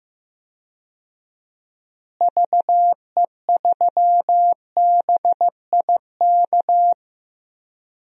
Morse code, heard as a keyed tone, about 15 wpm, 700 Hz. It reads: VE3BIK
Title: DXCC-100 Call Signs - 6 Characters - Repeated 3x - 15wpm